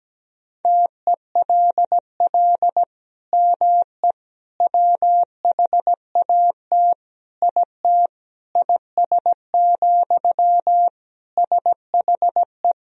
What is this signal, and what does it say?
Morse code, heard as a keyed tone, about 17 wpm, 700 Hz. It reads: TELL ME WHAT IT IS, SHE
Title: Sentences from Top 1000 Words - Review Part 1 - 17wpm